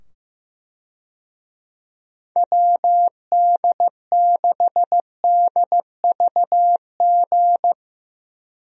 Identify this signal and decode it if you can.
Morse code, heard as a keyed tone, about 15 wpm, 700 Hz. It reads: WD6DVG
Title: USA Call Signs - 6 Characters - 15wpm